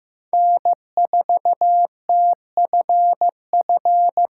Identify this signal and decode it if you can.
Morse code, heard as a keyed tone, about 15 wpm, 700 Hz. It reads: N4TFF
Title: USA Call Signs - 3x Character Spacing - 15wpm